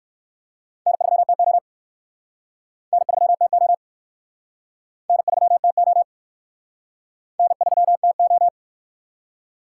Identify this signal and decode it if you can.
Morse code, heard as a keyed tone, about 59 wpm, 700 Hz. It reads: N3TO N3TO N3TO N3TO